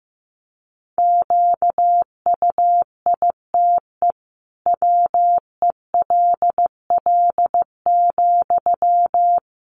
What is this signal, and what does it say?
Morse code, heard as a keyed tone, about 15 wpm, 700 Hz. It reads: QUITE WELL,
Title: Sentences from Top 700 Words 15wpm